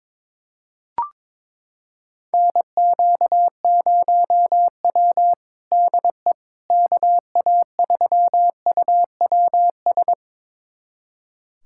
Morse code, heard as a keyed tone, about 22 wpm, 700 Hz. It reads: NQ0W DE KA3UWS
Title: USA Callsign DE Callsign 22wpm